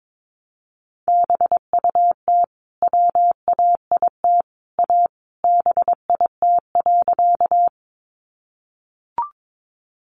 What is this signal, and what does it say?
Morse code, heard as a keyed tone, about 22 wpm, 700 Hz. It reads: BUT WAIT A BIT.
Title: Sentences from Top 900 Words 22wpm